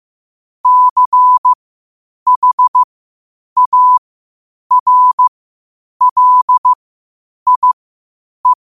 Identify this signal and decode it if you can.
Morse code, heard as a keyed tone, about 15 wpm, spaced as for 8 wpm, 1000 Hz. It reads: CHARLIE
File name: qotdfiles / 8wpm